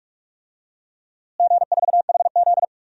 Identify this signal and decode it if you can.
Morse code, heard as a keyed tone, about 45 wpm, 700 Hz. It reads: G4HZ
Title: DXCC-100 Call Signs - 4 Characters - Repeated 3x - 45wpm